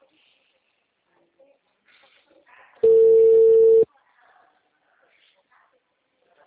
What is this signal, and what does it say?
Ringback tone (China, 450 Hz), with speech.
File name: out- -2032- - - .0